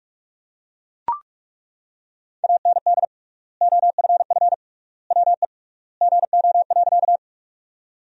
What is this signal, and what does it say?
Morse code, heard as a keyed tone, about 41 wpm, 700 Hz. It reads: AND OFF WE GO.
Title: 3x Word-Spacing - Sentences from Top 200 Words - 45wpm